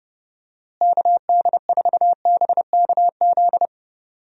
Morse code, heard as a keyed tone, about 30 wpm, 700 Hz. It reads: KD4BKZ
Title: USA Call Signs - 6 Characters - Repeated 3x - 30wpm